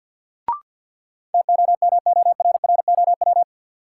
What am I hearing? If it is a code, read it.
Morse code, heard as a keyed tone, about 48 wpm, 700 Hz. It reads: TOMORROW
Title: Sentences from Top 1100 Words - New Word Drill Alt - 50wpm